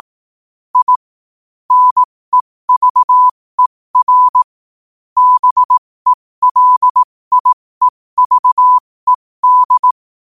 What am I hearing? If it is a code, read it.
Morse code, heard as a keyed tone, about 18 wpm, spaced as for 15 wpm, 1000 Hz. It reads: I NEVER BELIEVED
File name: qotdfiles / 15wpm